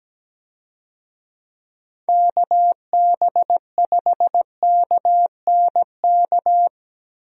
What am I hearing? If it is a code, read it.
Morse code, heard as a keyed tone, about 17 wpm, 700 Hz. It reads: KB5KNK
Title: USA Call Signs - 6 Characters - 17wpm